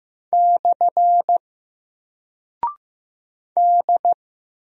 Morse code, heard as a keyed tone, about 15 wpm, 700 Hz. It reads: / D
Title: Single Character (Letter:Number:Punctuation:Prosign) Alt 15wpm